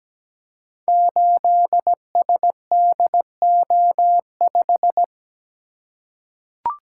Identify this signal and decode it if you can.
Morse code, heard as a keyed tone, about 17 wpm, 700 Hz. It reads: 8SDO5
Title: Code Groups - 5 Letter:Numbers - 17wpm